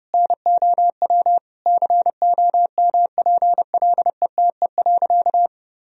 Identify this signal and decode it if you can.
Morse code, heard as a keyed tone, about 30 wpm, 700 Hz. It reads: NOW COMPLETE.
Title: Sentences from Top 400 Words and Abbreviations 30wpm